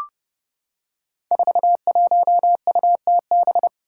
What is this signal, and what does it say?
Morse code, heard as a keyed tone, about 31 wpm, 700 Hz. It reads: 41UTB
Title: Five Challenging Letter-Number Combinations 30wpm